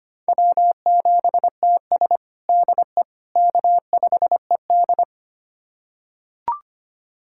Morse code, heard as a keyed tone, about 25 wpm, 700 Hz. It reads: W7TS DE K5ED
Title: USA Callsign DE Callsign - 4 Characters - 25wpm